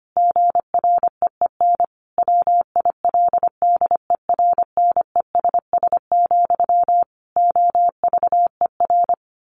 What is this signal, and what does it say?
Morse code, heard as a keyed tone, about 25 wpm, 700 Hz. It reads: GREEN WILDERNESS, OVER
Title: A Sound Of Thunder 25wpm